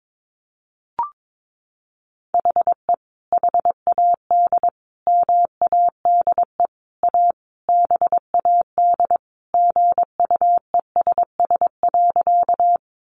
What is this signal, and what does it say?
Morse code, heard as a keyed tone, about 22 wpm, 700 Hz. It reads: HE HAD MADE A BAD GUESS.
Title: Sentences from Top 1500 Words 22wpm